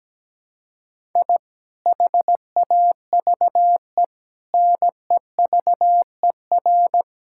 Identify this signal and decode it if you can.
Morse code, heard as a keyed tone, about 17 wpm, 700 Hz. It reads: I HAVE NEVER
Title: Sentences from Top 1000 Words - Review Part 5 - 17wpm